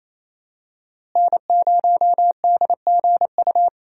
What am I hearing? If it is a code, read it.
Morse code, heard as a keyed tone, about 28 wpm, 700 Hz. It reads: N0DGU
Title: USA Call Signs - 5 Characters - Repeated 3x - 28wpm